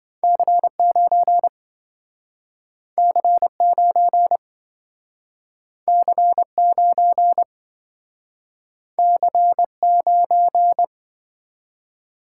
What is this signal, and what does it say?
Morse code, heard as a keyed tone, about 26 wpm, 700 Hz. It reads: C9 C9 C9 C9